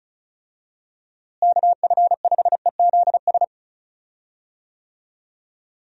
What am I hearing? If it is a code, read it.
Morse code, heard as a keyed tone, about 36 wpm, 700 Hz. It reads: KF5EZS
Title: USA Call Signs - 6 Characters - Repeated 3x - 35wpm